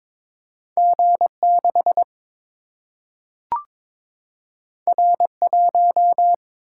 Morse code, heard as a keyed tone, about 22 wpm, 700 Hz. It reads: G 6 R 1
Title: Single-Letter Single-Number 22wpm